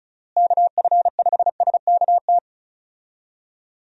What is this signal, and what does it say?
Morse code, heard as a keyed tone, about 34 wpm, 700 Hz. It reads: KF5SKT
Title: USA Call Signs - 6 Characters - Repeated 2x - 35wpm